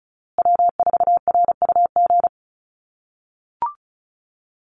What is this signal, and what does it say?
Morse code, heard as a keyed tone, about 34 wpm, 700 Hz. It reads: W4RUG
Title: USA Call Signs Repeated 2x 35wpm